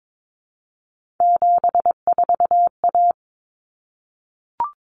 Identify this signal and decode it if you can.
Morse code, heard as a keyed tone, about 22 wpm, 700 Hz. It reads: 74A
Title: Common Cut Numbers - 3-Digit 22wpm